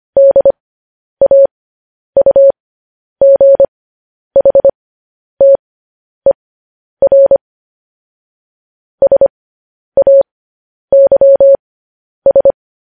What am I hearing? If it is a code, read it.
Morse code, heard as a keyed tone, about 25 wpm, spaced as for 10 wpm, 550 Hz. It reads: DAUGHTER SAYS
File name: podcast / News headlines-1x-25x10